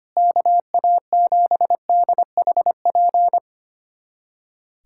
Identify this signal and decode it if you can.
Morse code, heard as a keyed tone, about 25 wpm, 700 Hz. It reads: KA7DHP
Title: USA Callsign DE Callsign 25wpm